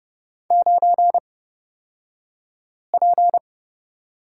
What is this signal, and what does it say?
Morse code, heard as a keyed tone, about 30 wpm, spaced as for 5 wpm, 690 Hz. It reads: 9P
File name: Lesson 36d - BT - Two Characters 30wpm